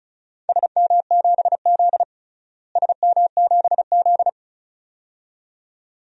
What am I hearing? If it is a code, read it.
Morse code, heard as a keyed tone, about 36 wpm, 700 Hz. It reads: SM7Z SM7Z
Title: DXCC-100 Call Signs - 4 Characters - Repeated 2x - 35wpm